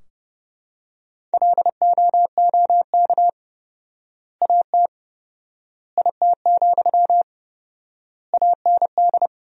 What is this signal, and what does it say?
Morse code, heard as a keyed tone, about 30 wpm, 700 Hz. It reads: LOOK AT IT, AND
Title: 4x Word-Spacing - Sentences from Top 200 Words - 30wpm